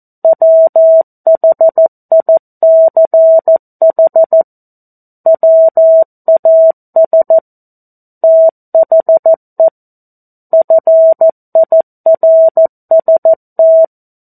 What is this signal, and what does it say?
Morse code, heard as a keyed tone, about 14 wpm, 650 Hz. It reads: WHICH WAS THE FIRST
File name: 02-Chapter-02 14 WPM